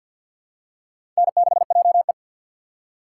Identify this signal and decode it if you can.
Morse code, heard as a keyed tone, about 52 wpm, 700 Hz. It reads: N6JE
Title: USA Call Signs - 4 Characters - 2x Character Spacing - 50wpm